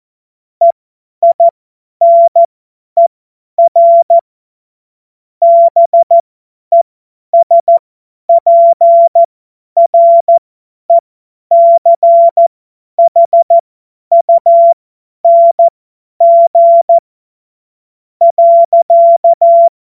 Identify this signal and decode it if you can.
Morse code, heard as a keyed tone, about 14 wpm, spaced as for 10 wpm, 680 Hz. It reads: EINER BESPRECHUNG.
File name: Morsekurs / Witze / Witz 23. Dezember 2025 50 F70